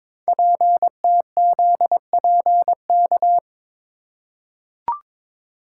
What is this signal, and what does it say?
Morse code, heard as a keyed tone, about 22 wpm, 700 Hz. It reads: PTZPK
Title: Code Groups - 5 Letters - 22wpm